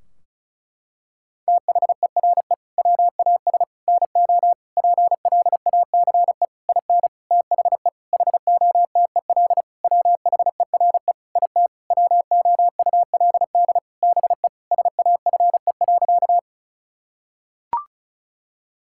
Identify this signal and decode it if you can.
Morse code, heard as a keyed tone, about 34 wpm, 700 Hz. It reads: THERE WAS NO PLACE IN THE HOTEL WHERE IT WOULD BE SAFE.
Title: Sentences from Top 1600 Words 35wpm